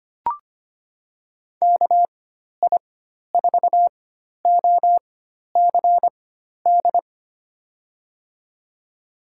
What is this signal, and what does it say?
Morse code, heard as a keyed tone, about 25 wpm, spaced as for 12 wpm, 700 Hz. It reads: KI4OCD
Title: USA Call Signs - 6 Characters - 4x Character Spacing - 25wpm